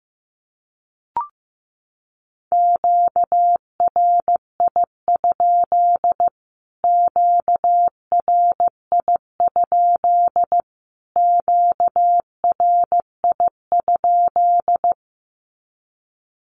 Morse code, heard as a keyed tone, about 15 wpm, 700 Hz. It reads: QRI? QRI? QRI?
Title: Q-Codes Repeated 3x 15wpm